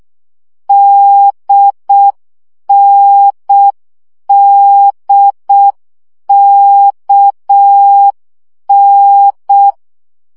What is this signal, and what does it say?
Morse code, heard as a keyed tone, about 6 wpm, 800 Hz. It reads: DNDKN
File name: CW gyakorlas n d k 5 baud